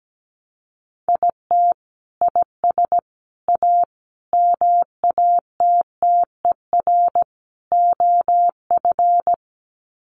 Morse code, heard as a keyed tone, about 17 wpm, 700 Hz. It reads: IT IS A MATTER OF
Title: Sets of 5 Words Encore2 17wpm